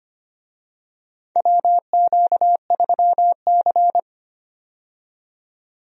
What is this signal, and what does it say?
Morse code, heard as a keyed tone, about 25 wpm, 700 Hz. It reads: WQ3C
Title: USA Call Signs - 4 Characters - 25wpm